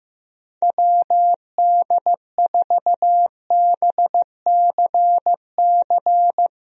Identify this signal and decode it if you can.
Morse code, heard as a keyed tone, about 15 wpm, 700 Hz. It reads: WD4BCC
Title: USA Call Signs - 6 Characters - Repeated 3x - 15wpm